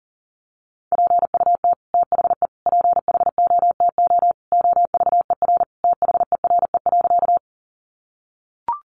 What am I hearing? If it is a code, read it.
Morse code, heard as a keyed tone, about 40 wpm, 700 Hz. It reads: PUT THE PHOTO OVER THERE.
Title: Sentences from Top 1200 Words 40wpm